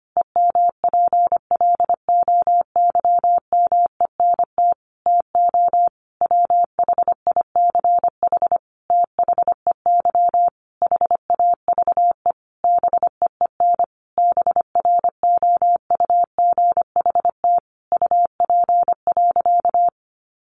Morse code, heard as a keyed tone, about 25 wpm, 700 Hz. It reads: EMPLOYMENT TO WHICH THEY HAVE BEEN BROUGHT UP.
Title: Sentences from Top 1500 Words 25wpm